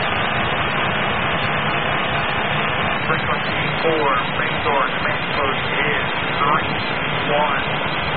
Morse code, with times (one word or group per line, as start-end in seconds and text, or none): none